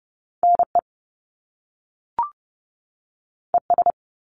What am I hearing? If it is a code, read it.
Morse code, heard as a keyed tone, about 30 wpm, 700 Hz. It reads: NE ES
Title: Top Two Letter Combinations in Words 30wpm